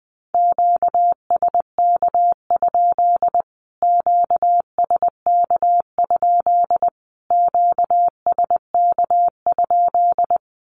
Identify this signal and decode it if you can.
Morse code, heard as a keyed tone, about 20 wpm, 700 Hz. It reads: QSK? QSK? QSK?
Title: Q-Codes Repeated 3x 20wpm